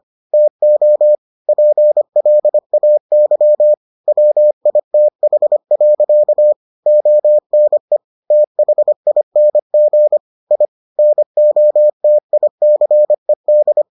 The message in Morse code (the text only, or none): TO PLAY WITH. ONE THING I NOTICED